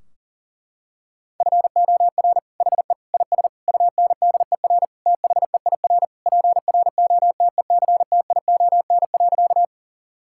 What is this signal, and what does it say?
Morse code, heard as a keyed tone, about 40 wpm, 700 Hz. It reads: FOR HE IS UNDER THEIR PROTECTION.
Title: Sentences from Top 1600 Words 40wpm